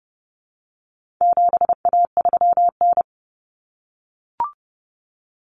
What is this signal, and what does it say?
Morse code, heard as a keyed tone, about 30 wpm, 700 Hz. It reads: 7A3N
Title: Common Cut Numbers - 4-Digit 30wpm